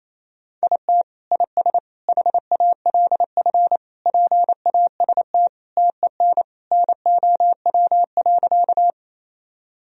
Morse code, heard as a keyed tone, about 28 wpm, 700 Hz. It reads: IT IS HALF PAST TEN NOW.
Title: Sentences from Top 1000 Words - Review Part 3 - 28wpm